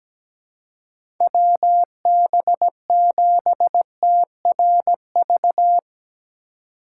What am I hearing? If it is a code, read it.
Morse code, heard as a keyed tone, about 17 wpm, 700 Hz. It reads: WB7TRV
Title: USA Call Signs - 6 Characters - 4x Character Spacing - 17wpm